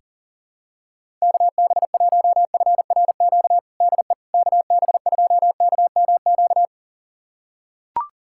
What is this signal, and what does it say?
Morse code, heard as a keyed tone, about 37 wpm, 700 Hz. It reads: KB1FRQ DE KB2KMQ
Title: USA Callsign DE Callsign - 6 Characters - 40wpm